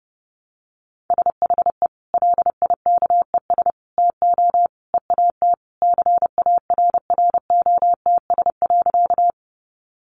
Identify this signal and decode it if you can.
Morse code, heard as a keyed tone, about 30 wpm, 700 Hz. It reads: SHE LIKES TO EAT CARROTS.
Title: Easy Phrases 30wpm